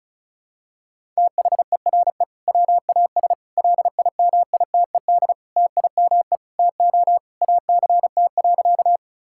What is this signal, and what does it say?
Morse code, heard as a keyed tone, about 36 wpm, 700 Hz. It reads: THERE WAS LIMITED TIME TO ACT.
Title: Sentences from Top 1000 Words - Review Part 6 - 35wpm